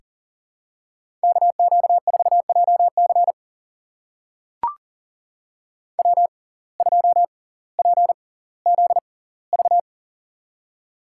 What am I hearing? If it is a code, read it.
Morse code, heard as a keyed tone, about 38 wpm, spaced as for 20 wpm, 700 Hz. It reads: KQ4JC W2PZV
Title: USA Call Signs - 5 Characters - 6x Character Spacing - 40wpm